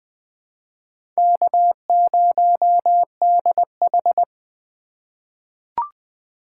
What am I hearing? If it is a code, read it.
Morse code, heard as a keyed tone, about 20 wpm, 700 Hz. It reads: K0DH
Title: USA Call Signs - 4 Characters - 8x Character Spacing - 20wpm